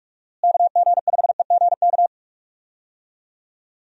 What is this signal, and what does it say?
Morse code, heard as a keyed tone, about 46 wpm, 700 Hz. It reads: KG5EGK
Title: USA Call Signs - 6 Characters - Repeated 2x - 45wpm